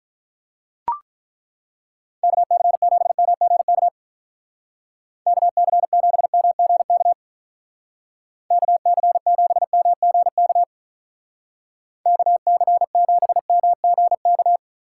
Morse code, heard as a keyed tone, about 43 wpm, 700 Hz. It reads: KC7MGK KC7MGK KC7MGK KC7MGK
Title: Speed Racing - USA 2x3 Call Signs 35wpm